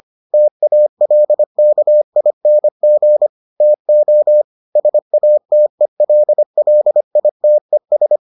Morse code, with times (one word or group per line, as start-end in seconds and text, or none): 0.3-3.3 TALKING
3.6-4.4 TO
4.8-8.2 SATELLITES